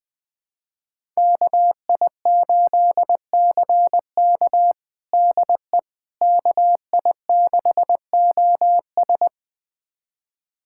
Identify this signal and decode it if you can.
Morse code, heard as a keyed tone, about 20 wpm, 700 Hz. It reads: KI8CK DE KI6OS
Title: USA Callsign DE Callsign - 5 Characters - 20wpm